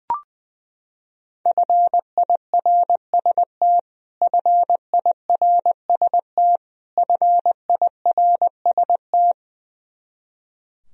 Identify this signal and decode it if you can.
Morse code, heard as a keyed tone, about 21 wpm, 700 Hz. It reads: FIRST FIRST FIRST
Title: Top 100 Words Repeated 20wpm